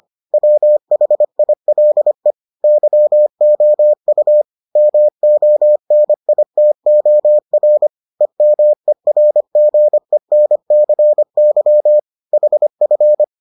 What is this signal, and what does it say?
Morse code, heard as a keyed tone, about 25 wpm, 600 Hz. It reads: WHILE YOU MONITOR EMERGENCY HF